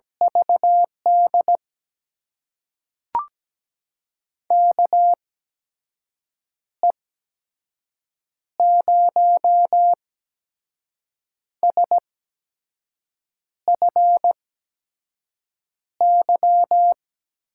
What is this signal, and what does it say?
Morse code, heard as a keyed tone, about 17 wpm, spaced as for 5 wpm, 700 Hz. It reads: VD KE0SFY
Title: USA Call Signs - 6 Characters - 8x Character Spacing - 17wpm